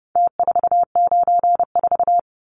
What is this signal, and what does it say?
Morse code, heard as a keyed tone, about 30 wpm, 700 Hz. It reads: T494
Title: Numbers - Contest Style - 4-Digit 30wpm